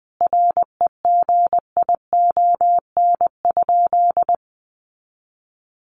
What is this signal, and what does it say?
Morse code, heard as a keyed tone, about 20 wpm, 700 Hz. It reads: REGION?
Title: Sentences from Top 1500 Words 20wpm